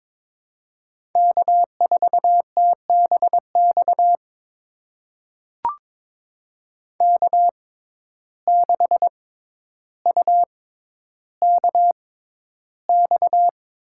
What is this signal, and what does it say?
Morse code, heard as a keyed tone, about 23 wpm, spaced as for 11 wpm, 700 Hz. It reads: K4TBX K6UKX
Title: USA Call Signs - 5 Characters - 6x Character Spacing - 22wpm